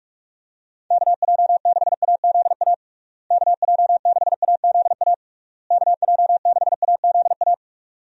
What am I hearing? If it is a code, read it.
Morse code, heard as a keyed tone, about 46 wpm, 700 Hz. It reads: KJ6AZA KJ6AZA KJ6AZA
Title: USA Call Signs - 6 Characters - Repeated 3x - 45wpm